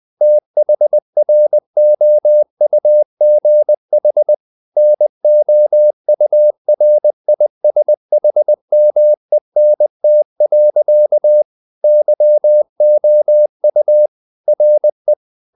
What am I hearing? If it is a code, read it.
Morse code, heard as a keyed tone, about 20 wpm, 600 Hz. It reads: THROUGH NOURISHMENT. YOU RE
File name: the old man and the sea part 2 20wpm